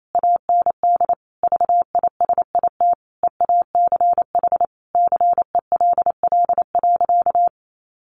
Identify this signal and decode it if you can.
Morse code, heard as a keyed tone, about 28 wpm, 700 Hz. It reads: AND VISIT EACH CELL.
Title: Sentences from Top 1100 Words 28wpm